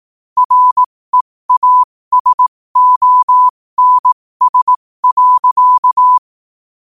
Morse code, heard as a keyed tone, about 18 wpm, spaced as for 15 wpm, 1000 Hz. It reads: REASONS.
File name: qotdfiles / 15wpm